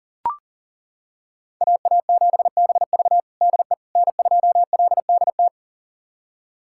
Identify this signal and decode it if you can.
Morse code, heard as a keyed tone, about 37 wpm, 700 Hz. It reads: AA7BV DE N2LDT
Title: USA Callsign DE Callsign - 5 Characters - 40wpm